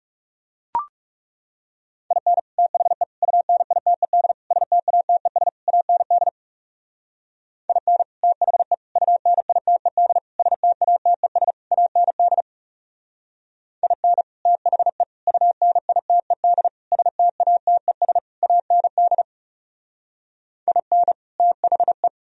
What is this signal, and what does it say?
Morse code, heard as a keyed tone, about 36 wpm, 700 Hz. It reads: IN THE UNITED STATES AND IN THE UNITED STATES AND IN THE UNITED STATES AND IN THE
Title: Speed-Racing - Sets of 5 Words 30wpm